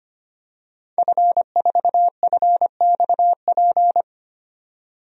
Morse code, heard as a keyed tone, about 25 wpm, 700 Hz. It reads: F4FXP
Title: DXCC-100 Call Signs - 5 Characters - 25wpm